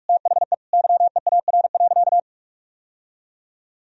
Morse code, heard as a keyed tone, about 46 wpm, 700 Hz. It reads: THE YEAR.
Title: Sentences from Top 1000 Words - Review Part 1 - 45wpm